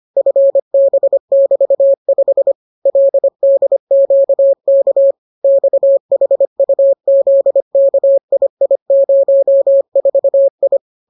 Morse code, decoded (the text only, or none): FB<BT>5 LDQK XHUZKII04I